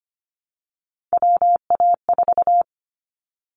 Morse code, heard as a keyed tone, about 25 wpm, 700 Hz. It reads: WA4
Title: Callsign Prefixes for Top 25 DXCC - Repeated 3x - 25wpm